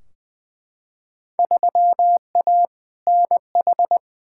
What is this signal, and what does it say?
Morse code, heard as a keyed tone, about 21 wpm, 700 Hz. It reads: 3A NH
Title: Field Day Exchanges Encore - 20wpm